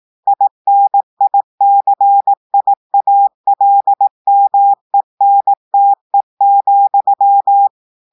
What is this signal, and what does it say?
Morse code, heard as a keyed tone, about 18 wpm, 800 Hz. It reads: INICIALMENTE,
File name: portugais texte en clair 2 vitesse 18 mots minutes